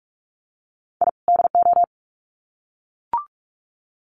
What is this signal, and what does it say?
Morse code, heard as a keyed tone, about 44 wpm, 700 Hz. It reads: I DO
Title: Sets of 2 Words 45wpm